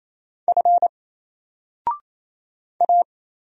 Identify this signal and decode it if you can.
Morse code, heard as a keyed tone, about 27 wpm, 700 Hz. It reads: F A
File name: Single-Letter-Number-28wpm